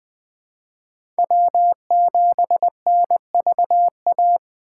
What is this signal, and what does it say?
Morse code, heard as a keyed tone, about 20 wpm, 700 Hz. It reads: W7NVA
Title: USA Call Signs - 4x Character Spacing - 20wpm